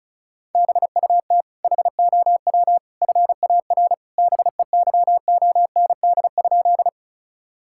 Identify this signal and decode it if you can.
Morse code, heard as a keyed tone, about 36 wpm, 700 Hz. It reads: BUT HOW FAR BEYOND?
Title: Sentences from Top 1000 Words - Review Part 6 - 35wpm